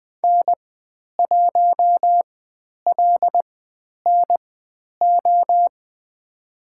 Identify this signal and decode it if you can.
Morse code, heard as a keyed tone, about 20 wpm, spaced as for 10 wpm, 700 Hz. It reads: N1LNO
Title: 095 Lesson 16g - Letter c - Call Signs 20wpm